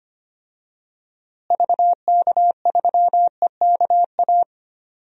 Morse code, heard as a keyed tone, about 25 wpm, 700 Hz. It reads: VK3EKA
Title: DXCC-100 Call Signs - 6 Characters - Repeated 2x - 25wpm